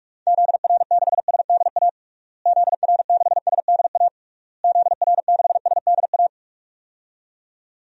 Morse code, heard as a keyed tone, about 46 wpm, 700 Hz. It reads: ZR6SDA ZR6SDA ZR6SDA
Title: DXCC-100 Call Signs - 6 Characters - Repeated 3x - 45wpm